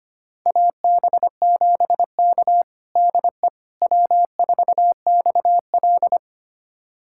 Morse code, heard as a keyed tone, about 25 wpm, 700 Hz. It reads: AB7K DE W4XL
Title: USA Callsign DE Callsign - 4 Characters - 25wpm